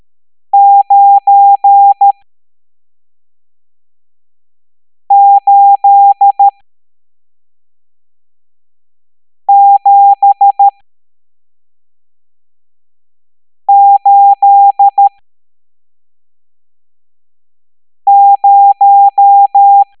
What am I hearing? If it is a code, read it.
Morse code, heard as a keyed tone, about 13 wpm, spaced as for 3 wpm, 800 Hz. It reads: 98780
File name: hamdata / L11-EX01-67890